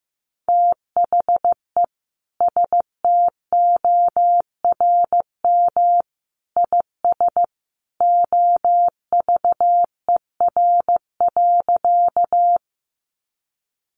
Morse code, heard as a keyed tone, about 15 wpm, 700 Hz. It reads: THE STORM IS OVER.
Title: Sentences from Top 1500 Words 15wpm